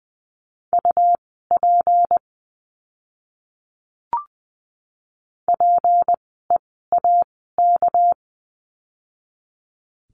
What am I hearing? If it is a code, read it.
Morse code, heard as a keyed tone, about 20 wpm, spaced as for 14 wpm, 700 Hz. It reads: UP PEAK
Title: 199 Lesson 32f - Letter k - Words 20wpm